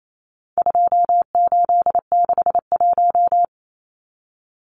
Morse code, heard as a keyed tone, about 28 wpm, 700 Hz. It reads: 2861
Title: Numbers - 4-Digit 28wpm